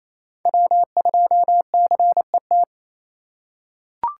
Morse code, heard as a keyed tone, about 28 wpm, 700 Hz. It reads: W2CET
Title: USA Call Signs - 5 Characters - 4x Character Spacing - 28wpm